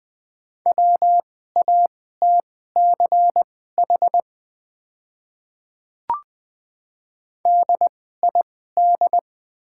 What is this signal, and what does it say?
Morse code, heard as a keyed tone, about 20 wpm, spaced as for 14 wpm, 700 Hz. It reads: WATCH DID
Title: 2x Character Spacing - Top 300 Words - 20wpm